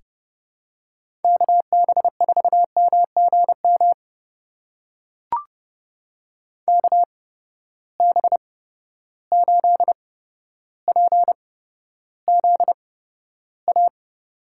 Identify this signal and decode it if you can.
Morse code, heard as a keyed tone, about 31 wpm, spaced as for 13 wpm, 700 Hz. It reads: KB4MGM KB8PZA
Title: USA Call Signs - 6 Characters - 8x Character Spacing - 30wpm